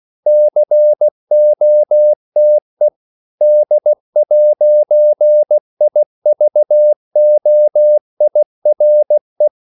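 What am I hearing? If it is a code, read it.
Morse code, heard as a keyed tone, about 16 wpm, 600 Hz. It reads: COTE D'IVOIRE
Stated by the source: CW practice of national capitals at 16 words per minute, from ON6ZQ.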